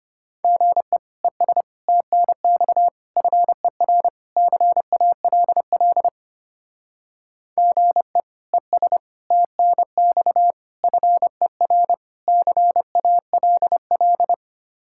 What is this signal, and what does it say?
Morse code, heard as a keyed tone, about 32 wpm, 700 Hz. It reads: GE ES TNX FER CALL GE ES TNX FER CALL